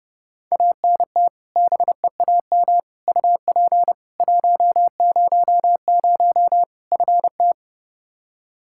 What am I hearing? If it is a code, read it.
Morse code, heard as a keyed tone, about 32 wpm, 700 Hz. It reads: ANT BEAM UP 100 FT
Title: K1USN SST QSO Practice 30wpm